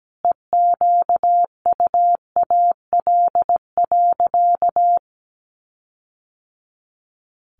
Morse code, heard as a keyed tone, about 17 wpm, 700 Hz. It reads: EQUAL.
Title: Sentences from Top 1400 Words 17wpm